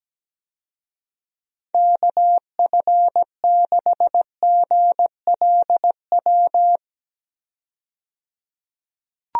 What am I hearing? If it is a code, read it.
Morse code, heard as a keyed tone, about 17 wpm, 700 Hz. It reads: KF6GLW
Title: USA Call Signs - 6 Characters - 17wpm